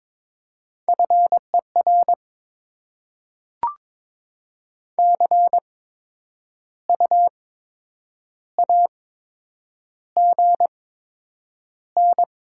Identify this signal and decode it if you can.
Morse code, heard as a keyed tone, about 22 wpm, spaced as for 8 wpm, 700 Hz. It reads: FER CU AGN
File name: Top 20 QSO Elements - 8x Character Spacing - 22wpm